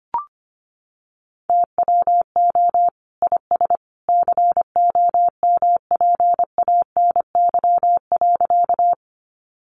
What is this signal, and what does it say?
Morse code, heard as a keyed tone, about 25 wpm, 700 Hz. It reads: TWO IS COMPANY.
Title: Sentences from Top 600 Words 25wpm